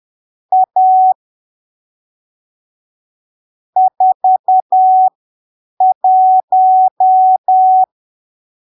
Morse code, heard as a keyed tone, about 10 wpm, spaced as for 7 wpm, 750 Hz. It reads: A 41
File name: arquivos / texto / texto 12 10ppm